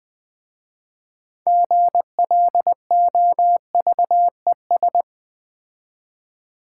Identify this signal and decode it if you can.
Morse code, heard as a keyed tone, about 20 wpm, 700 Hz. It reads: GLOVES
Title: Difficult Words with Hints - 20wpm